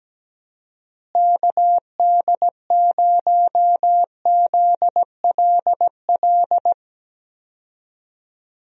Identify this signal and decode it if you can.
Morse code, heard as a keyed tone, about 17 wpm, 700 Hz. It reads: KD0ZLL
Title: USA Call Signs - 6 Characters - Repeated 2x - 17wpm